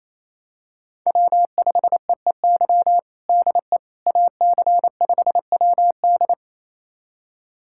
Morse code, heard as a keyed tone, about 28 wpm, 700 Hz. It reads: W5EEY DE AC5WD
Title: USA Callsign DE Callsign - 5 Characters - 28wpm